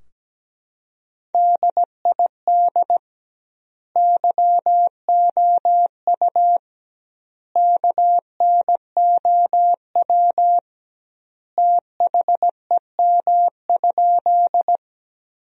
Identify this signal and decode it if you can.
Morse code, heard as a keyed tone, about 17 wpm, 700 Hz. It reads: DID YOU KNOW THEM?
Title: 2x Word-Spacing - Sentences from Top 100 Words - 17wpm